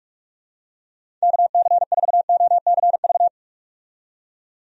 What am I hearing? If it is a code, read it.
Morse code, heard as a keyed tone, about 46 wpm, 700 Hz. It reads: KC4OCV
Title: USA Call Signs - 6 Characters - 45wpm